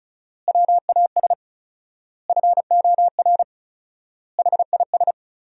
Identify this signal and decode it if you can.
Morse code, heard as a keyed tone, about 34 wpm, 700 Hz. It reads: WAS FOR HIS
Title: 4x Word-Spacing - Sentences from Top 500 Words - 35wpm